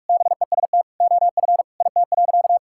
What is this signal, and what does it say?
Morse code, heard as a keyed tone, about 46 wpm, 690 Hz. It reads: BEST OF IT.
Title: Sentences from Top 1000 Words - Review Part 5 - 45wpm